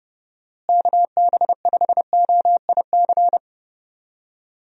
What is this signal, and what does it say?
Morse code, heard as a keyed tone, about 29 wpm, 700 Hz. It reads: KB5OIC
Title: USA Call Signs - 6 Characters - Repeated 2x - 30wpm